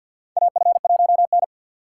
Morse code, heard as a keyed tone, about 52 wpm, 700 Hz. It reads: AF1N
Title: USA Call Signs - 4 Characters - 6x Character Spacing - 50wpm